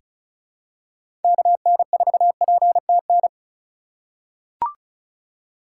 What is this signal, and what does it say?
Morse code, heard as a keyed tone, about 34 wpm, 690 Hz. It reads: KN4PTN
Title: USA Call Signs - 4x Character Spacing - 35wpm